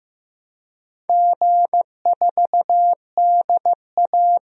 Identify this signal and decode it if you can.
Morse code, heard as a keyed tone, about 15 wpm, 700 Hz. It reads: G4DA
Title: DXCC-100 Call Signs - 4 Characters - 15wpm